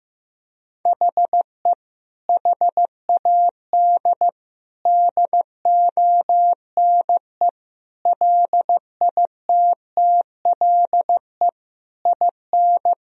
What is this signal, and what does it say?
Morse code, heard as a keyed tone, about 15 wpm, 700 Hz. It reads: HE HAD DONE LITTLE IN
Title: Sentences from Top 1000 Words - Review Part 3 - 15wpm